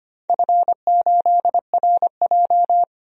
Morse code, heard as a keyed tone, about 25 wpm, 700 Hz. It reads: F8RJ
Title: DXCC-100 Call Signs - 4 Characters - 25wpm